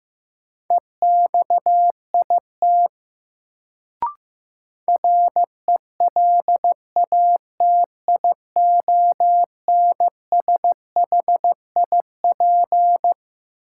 Morse code, heard as a keyed tone, about 15 wpm, 700 Hz. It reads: EXIT RELATIONSHIP
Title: Sentences from Top 1100 Words - New Word Drill Alt - 15wpm